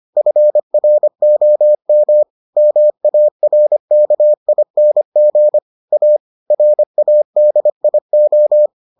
Morse code, heard as a keyed tone, about 25 wpm, 600 Hz. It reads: FROM MARKING A RADIO